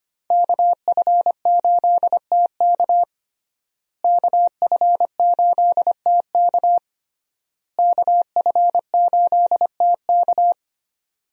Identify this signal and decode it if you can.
Morse code, heard as a keyed tone, about 25 wpm, 700 Hz. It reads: KF8TK KF8TK KF8TK
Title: USA Call Signs - 5 Characters - Repeated 3x - 25wpm